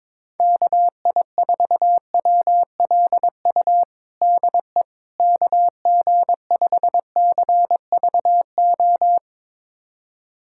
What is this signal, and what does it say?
Morse code, heard as a keyed tone, about 22 wpm, 700 Hz. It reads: KI4WLU DE KG5CVO
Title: USA Callsign DE Callsign - 6 Characters - 22wpm